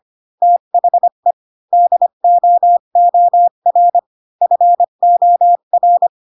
Morse code, heard as a keyed tone, about 25 wpm, 700 Hz. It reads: THE DOOR FOR
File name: Pinocchio / Pinocchio ch7 20 25 ik1hge